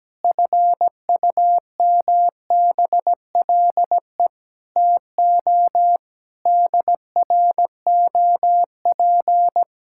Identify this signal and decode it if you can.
Morse code, heard as a keyed tone, about 17 wpm, 700 Hz. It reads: FUMBLE TO DROP
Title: Pangram Challenge v2 - 17wpm